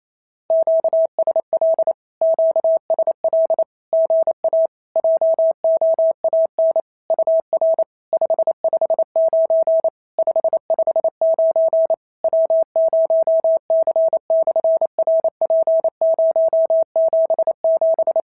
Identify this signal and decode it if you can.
Morse code, heard as a keyed tone, about 28 wpm, 650 Hz. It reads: QSL QSL GA JOAN UR 559 559 W0C/RP077
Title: SOTA QSO Practice 28wpm